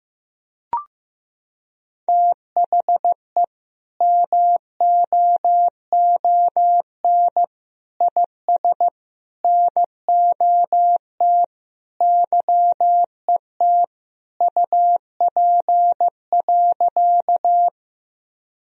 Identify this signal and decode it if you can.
Morse code, heard as a keyed tone, about 15 wpm, 700 Hz. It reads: THE MOON IS NOT YET UP.
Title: Sentences from Top 1000 Words - Review Part 3 - 15wpm